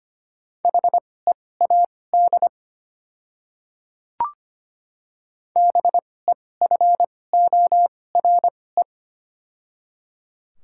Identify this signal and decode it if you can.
Morse code, heard as a keyed tone, about 25 wpm, spaced as for 18 wpm, 700 Hz. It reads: HEAD BEFORE
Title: 2x Character Spacing - Top 500 Words - 25wpm